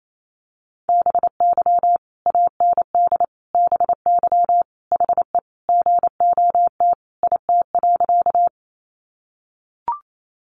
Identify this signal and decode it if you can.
Morse code, heard as a keyed tone, about 28 wpm, 700 Hz. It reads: BY AND BY HE GOT IT.
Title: Sentences from Top 300 Words 28wpm